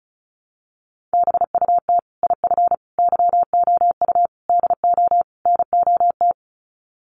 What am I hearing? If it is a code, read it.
Morse code, heard as a keyed tone, about 34 wpm, 700 Hz. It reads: BUT IF YOU DO NOT
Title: Sets of 5 Words Encore 35wpm